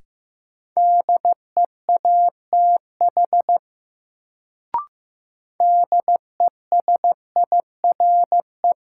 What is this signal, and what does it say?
Morse code, heard as a keyed tone, about 15 wpm, 700 Hz. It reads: DEATH DESIRE
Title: Sentences from Top 1100 Words - New Word Drill - 15wpm